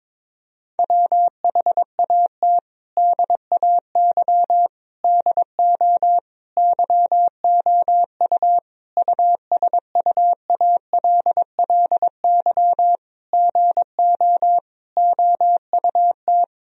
Morse code, heard as a keyed tone, about 22 wpm, 700 Hz. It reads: WHAT DAY DO YOU USUALLY GO OUT
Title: Sentences from Top 1000 Words - Review Part 3 - 22wpm